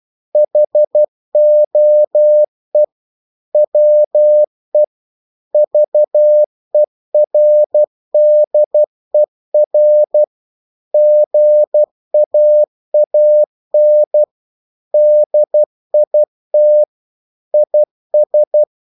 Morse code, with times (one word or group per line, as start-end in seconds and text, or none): 0.4-2.8 HOE
3.5-4.8 WE
5.5-10.2 VERDER
10.9-14.2 GAAN
14.9-16.8 DIT
17.5-18.6 IS